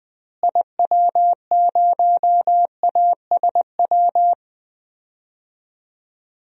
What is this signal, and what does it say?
Morse code, heard as a keyed tone, about 20 wpm, 700 Hz. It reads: IW0ASW
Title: DXCC-100 Call Signs - 6 Characters - Repeated 2x - 20wpm